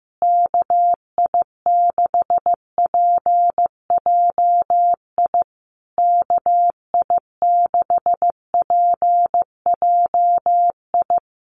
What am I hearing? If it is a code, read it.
Morse code, heard as a keyed tone, about 14 wpm, 700 Hz. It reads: KI6PJI KI6PJI
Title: USA Call Signs Repeated 3x 15wpm